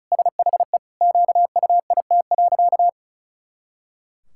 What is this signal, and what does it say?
Morse code, heard as a keyed tone, about 34 wpm, 700 Hz. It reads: SHE QUIT.
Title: Pangram Challenge v2 - 35wpm